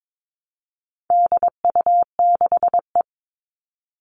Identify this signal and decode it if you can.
Morse code, heard as a keyed tone, about 22 wpm, 700 Hz. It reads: DU6E
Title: All Cut Numbers - 4-Digit 22wpm